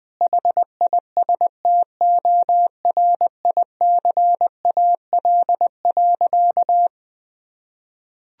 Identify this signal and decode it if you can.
Morse code, heard as a keyed tone, about 20 wpm, 700 Hz. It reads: HISTORICAL.
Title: Sentences from Top 1000 Words - Review Part 4 - 20wpm